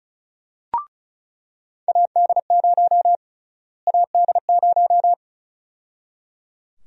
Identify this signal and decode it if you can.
Morse code, heard as a keyed tone, about 34 wpm, 700 Hz. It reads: AD0 AD0
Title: USA Call Sign Prefixes - Repeated 2x - 35wpm